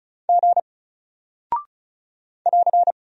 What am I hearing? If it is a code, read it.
Morse code, heard as a keyed tone, about 36 wpm, 700 Hz. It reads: G <AR>
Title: Single Character (Letter:Number:Punctuation:Prosign) Alt 35wpm